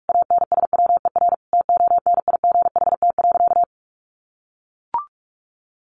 Morse code, heard as a keyed tone, about 44 wpm, 700 Hz. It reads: ANSWER TONIGHT.
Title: Sentences from Top 600 Words 45wpm